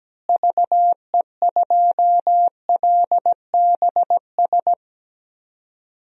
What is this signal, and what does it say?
Morse code, heard as a keyed tone, about 17 wpm, 700 Hz. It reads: VE2LBS
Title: DXCC-100 Call Signs - 6 Characters - Repeated 2x - 17wpm